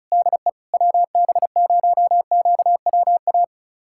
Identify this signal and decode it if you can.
Morse code, heard as a keyed tone, about 36 wpm, 700 Hz. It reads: DE WB0QWA
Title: USA Callsign DE Callsign 35wpm